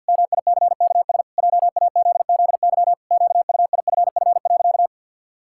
Morse code, heard as a keyed tone, about 50 wpm, 700 Hz. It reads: MICKS JAZZY QUIFF.
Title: Pangram Challenge v2 - 50wpm